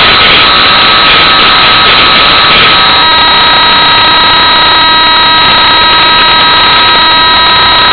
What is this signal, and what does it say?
PSK31, about 31.25 baud.